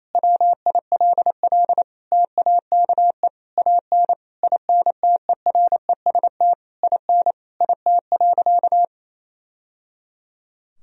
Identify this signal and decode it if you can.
Morse code, heard as a keyed tone, about 27 wpm, 700 Hz. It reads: WILL TAKE AN INTEREST IN IT.
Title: Sentences from Top 1000 Words - Review Part 3 - 28wpm